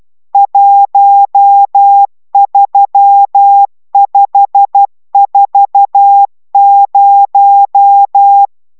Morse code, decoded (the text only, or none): 13540